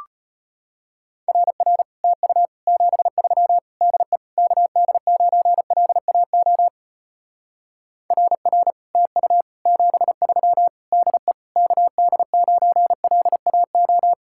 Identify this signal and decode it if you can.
Morse code, heard as a keyed tone, about 33 wpm, 700 Hz. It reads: RR TU 73 DE KD9LAO RR TU 73 DE KD9LAO